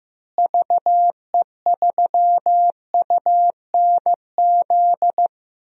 Morse code, heard as a keyed tone, about 15 wpm, 700 Hz. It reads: VE3UNZ
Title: DXCC-100 Call Signs - 6 Characters - Repeated 2x - 15wpm